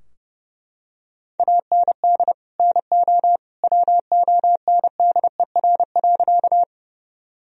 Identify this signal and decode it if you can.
Morse code, heard as a keyed tone, about 32 wpm, 700 Hz. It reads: AND NO WONDER.
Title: Sentences from Top 1000 Words - Review Part 3 - 30wpm